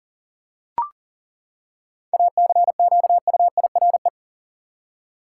Morse code, heard as a keyed tone, about 40 wpm, 700 Hz. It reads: ACQUIRE
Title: Seven Letter Words - 40wpm